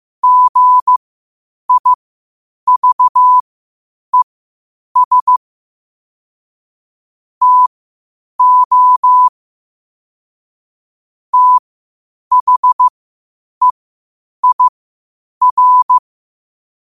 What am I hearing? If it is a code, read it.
Morse code, heard as a keyed tone, about 15 wpm, spaced as for 8 wpm, 1000 Hz. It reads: GIVES TO THEIR
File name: qotdfiles / 8wpm